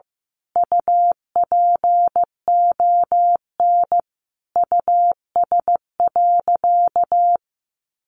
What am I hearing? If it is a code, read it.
Morse code, heard as a keyed tone, about 15 wpm, 700 Hz. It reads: UPON US.
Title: Sentences from Top 1500 Words 15wpm